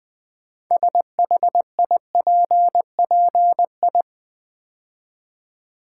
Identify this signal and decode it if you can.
Morse code, heard as a keyed tone, about 20 wpm, 700 Hz. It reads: SHIPPI
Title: Build A Word - 2 Letters At A Time - 8-Letter Words - 20wpm